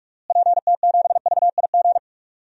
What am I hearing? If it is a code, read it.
Morse code, heard as a keyed tone, about 46 wpm, 700 Hz. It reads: PT7VIG
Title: DXCC-100 Call Signs - 6 Characters - Repeated 3x - 45wpm